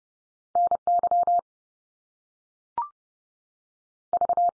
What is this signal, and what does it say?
Morse code, heard as a keyed tone, about 30 wpm, 700 Hz. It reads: NY V
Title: US States 30WPM